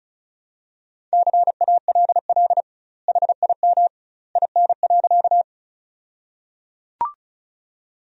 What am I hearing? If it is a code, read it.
Morse code, heard as a keyed tone, about 34 wpm, 700 Hz. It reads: CALL HIM IN.
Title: 2x Word-Spacing - Sentences from Top 100 Words - 35wpm